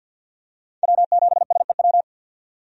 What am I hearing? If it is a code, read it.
Morse code, heard as a keyed tone, about 49 wpm, 700 Hz. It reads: W7SEW
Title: USA Call Signs - 5 Characters - 50wpm